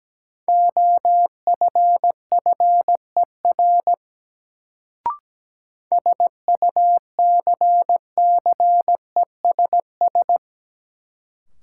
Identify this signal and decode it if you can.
Morse code, heard as a keyed tone, about 17 wpm, 700 Hz. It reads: OFFER SUCCESS
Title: Sentences from Top 900 Words - New Word Drill - 17wpm